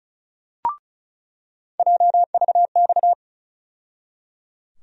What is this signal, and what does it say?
Morse code, heard as a keyed tone, about 36 wpm, 700 Hz. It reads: JVX
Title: USA Call Sign Postfixes - 3 Letters 35wpm